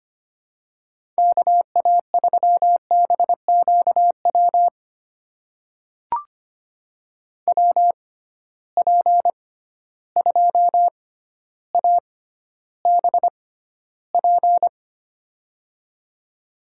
Morse code, heard as a keyed tone, about 25 wpm, spaced as for 13 wpm, 700 Hz. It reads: KA3BQW WP2ABP
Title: USA Call Signs - 6 Characters - 6x Character Spacing - 25wpm